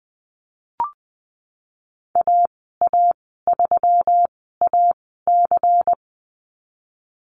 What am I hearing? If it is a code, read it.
Morse code, heard as a keyed tone, about 20 wpm, spaced as for 15 wpm, 700 Hz. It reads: AA3AC
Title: 128 Lesson 21g - Number 6 - Call Signs 20wpm